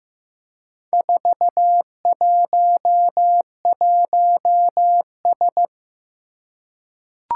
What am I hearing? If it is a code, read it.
Morse code, heard as a keyed tone, about 15 wpm, 700 Hz. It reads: 411S
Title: Four Challenging Letter-Number Combinations 15wpm